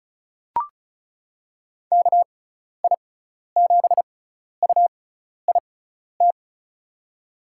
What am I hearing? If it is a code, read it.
Morse code, heard as a keyed tone, about 36 wpm, spaced as for 12 wpm, 700 Hz. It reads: KI7UIT
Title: USA Call Signs - 6 Characters - 6x Character Spacing - 35wpm